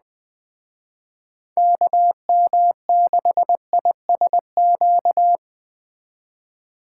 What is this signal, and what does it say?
Morse code, heard as a keyed tone, about 20 wpm, 700 Hz. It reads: KM6ISQ